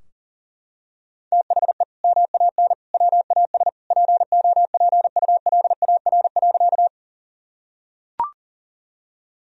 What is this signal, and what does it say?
Morse code, heard as a keyed tone, about 40 wpm, 700 Hz. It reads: THE MAN WAS POPULAR.
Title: Sentences from Top 1000 Words - Review Part 3 - 40wpm